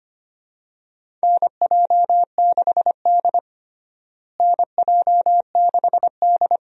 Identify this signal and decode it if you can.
Morse code, heard as a keyed tone, about 25 wpm, 700 Hz. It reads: NJ6D NJ6D
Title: USA Call Signs - 4 Characters - Repeated 2x - 25wpm